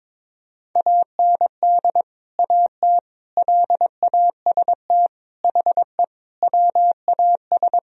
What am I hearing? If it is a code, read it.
Morse code, heard as a keyed tone, about 22 wpm, 700 Hz. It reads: AND AT LAST HE WAS
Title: Sentences from Top 1000 Words - Review Part 2 - 22wpm